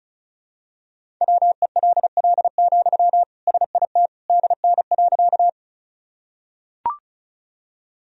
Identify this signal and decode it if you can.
Morse code, heard as a keyed tone, about 36 wpm, 700 Hz. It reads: WELL, SIT DN.
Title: Sentences from Top 500 Words and Abbreviations 35wpm